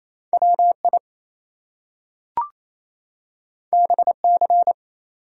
Morse code, heard as a keyed tone, about 28 wpm, 700 Hz. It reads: WI BC
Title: US States and Canadian Provinces 28wpm